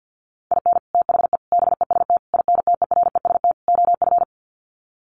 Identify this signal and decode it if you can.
Morse code, heard as a keyed tone, about 50 wpm, 700 Hz. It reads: IN THE BEST INTEREST OF
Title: Sets of 5 Words Encore 50wpm